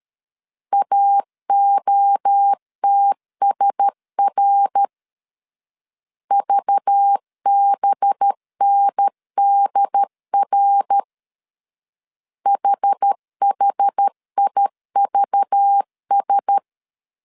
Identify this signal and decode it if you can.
Morse code, heard as a keyed tone, about 13 wpm, 800 Hz. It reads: AOTSR VBNDR HHIVS
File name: CW / DET / 1 6 1500C S A 12wpm 3440